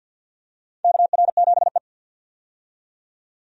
Morse code, heard as a keyed tone, about 46 wpm, 700 Hz. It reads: KR7E
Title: USA Call Signs - 4 Characters - 50wpm